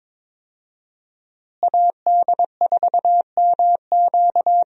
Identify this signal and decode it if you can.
Morse code, heard as a keyed tone, about 22 wpm, 700 Hz. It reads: AD4MQ
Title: USA Call Signs - 5 Characters - Repeated 3x - 22wpm